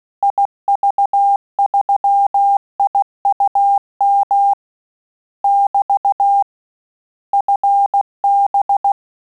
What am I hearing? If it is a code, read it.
Morse code, heard as a keyed tone, about 16 wpm, 800 Hz. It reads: IV3IUM <BT> FB
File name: Telegraphy-group / QSO CW 15 PPM